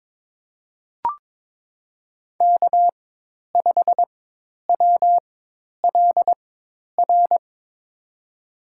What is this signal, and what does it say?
Morse code, heard as a keyed tone, about 22 wpm, spaced as for 10 wpm, 700 Hz. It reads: K5WLR
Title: USA Call Signs - 5 Characters - 4x Character Spacing - 22wpm